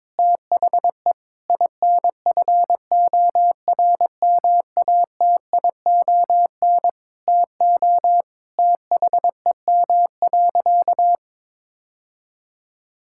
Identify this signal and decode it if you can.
Morse code, heard as a keyed tone, about 22 wpm, 700 Hz. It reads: THE INFORMATION TO THEM.
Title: Sentences from Top 1000 Words - Review Part 5 - 22wpm